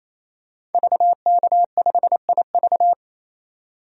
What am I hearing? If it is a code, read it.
Morse code, heard as a keyed tone, about 27 wpm, 700 Hz. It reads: VK5IV
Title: DXCC-100 Call Signs - 5 Characters - 28wpm